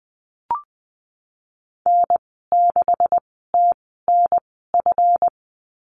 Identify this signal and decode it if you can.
Morse code, heard as a keyed tone, about 20 wpm, spaced as for 14 wpm, 700 Hz. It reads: N6TNF
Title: 245 Lesson 39g - Letter z - Call Signs 20wpm